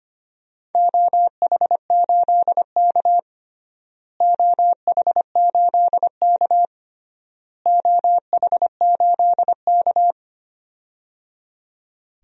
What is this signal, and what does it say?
Morse code, heard as a keyed tone, about 25 wpm, 700 Hz. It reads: OH8K OH8K OH8K
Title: DXCC-100 Call Signs - 4 Characters - Repeated 3x - 25wpm